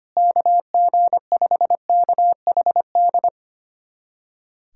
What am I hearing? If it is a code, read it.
Morse code, heard as a keyed tone, about 25 wpm, 700 Hz. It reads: KG5KHD
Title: USA Callsign DE Callsign 25wpm